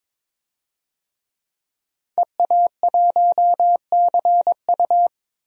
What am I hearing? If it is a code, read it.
Morse code, heard as a keyed tone, about 22 wpm, 700 Hz. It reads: EA1CU
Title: DXCC-100 Call Signs - 5 Characters - 22wpm